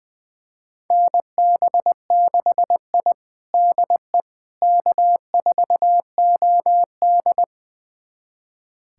Morse code, heard as a keyed tone, about 20 wpm, 700 Hz. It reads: NB6I DE K4OD
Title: USA Callsign DE Callsign - 4 Characters - 20wpm